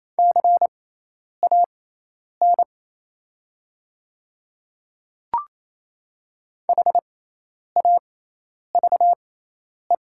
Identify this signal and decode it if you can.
Morse code, heard as a keyed tone, about 28 wpm, spaced as for 10 wpm, 700 Hz. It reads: CAN HAVE
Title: 6x Character Spacing - Top 100 Words - 28wpm